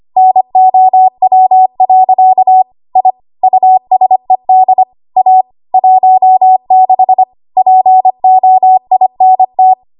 Morse code, heard as a keyed tone, about 25 wpm, 750 Hz. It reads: NOW. I USED A 16 POINT
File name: Archive / 25 WPM / 25WPM